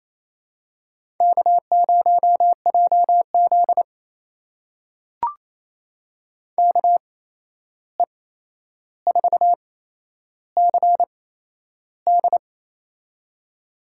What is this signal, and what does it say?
Morse code, heard as a keyed tone, about 28 wpm, spaced as for 11 wpm, 700 Hz. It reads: K0JZ KE4CD
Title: USA Call Signs - 8x Character Spacing - 28wpm